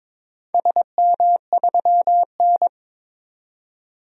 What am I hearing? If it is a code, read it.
Morse code, heard as a keyed tone, about 22 wpm, 700 Hz. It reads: SM3N
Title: DXCC-100 Call Signs - 4 Characters - 22wpm